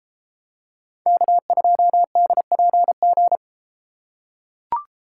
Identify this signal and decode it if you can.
Morse code, heard as a keyed tone, about 31 wpm, 700 Hz. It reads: K2DPG